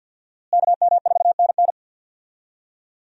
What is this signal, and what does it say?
Morse code, heard as a keyed tone, about 49 wpm, 700 Hz. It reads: KM4NN
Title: USA Call Signs - 5 Characters - Repeated 2x - 50wpm